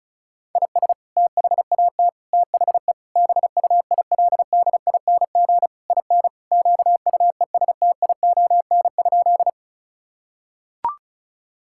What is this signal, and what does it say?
Morse code, heard as a keyed tone, about 34 wpm, 700 Hz. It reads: IS THAT THE BUILDING IN QUESTION?
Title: Sentences from Top 1000 Words - Review Part 5 - 35wpm